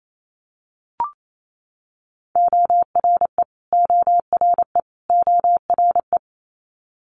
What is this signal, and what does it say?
Morse code, heard as a keyed tone, about 28 wpm, 700 Hz. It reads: ORE ORE ORE
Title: Multi-letter Phonemes Repeated 3x 28wpm